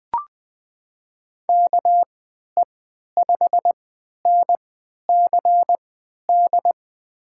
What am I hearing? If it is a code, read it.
Morse code, heard as a keyed tone, about 20 wpm, spaced as for 11 wpm, 700 Hz. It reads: KE5NCD
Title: USA Call Signs - 6 Characters - 3x Character Spacing - 20wpm